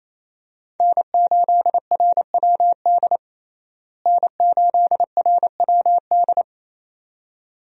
Morse code, heard as a keyed tone, about 28 wpm, 700 Hz. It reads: N8RWD N8RWD
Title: USA Call Signs - 5 Characters - Repeated 2x - 28wpm